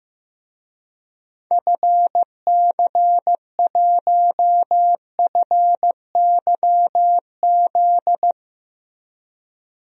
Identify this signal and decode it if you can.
Morse code, heard as a keyed tone, about 15 wpm, 700 Hz. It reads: FC1FYZ
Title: DXCC-100 Call Signs - 6 Characters - 15wpm